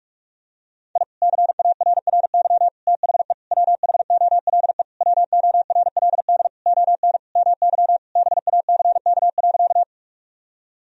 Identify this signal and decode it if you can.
Morse code, heard as a keyed tone, about 43 wpm, 700 Hz. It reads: I CARRY THE WHOLE WORLD ON MY BACK.
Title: Sentences from Top 1000 Words - Review Part 1 - 45wpm